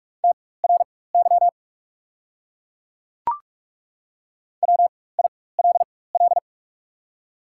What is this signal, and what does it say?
Morse code, heard as a keyed tone, about 43 wpm, spaced as for 21 wpm, 690 Hz. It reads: TRY WILL
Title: 4x Character Spacing - Top 300 Words - 45wpm